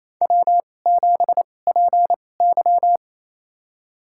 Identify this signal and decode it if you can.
Morse code, heard as a keyed tone, about 28 wpm, spaced as for 20 wpm, 700 Hz. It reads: W7PY
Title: USA Call Signs - 2x Character Spacing - 28wpm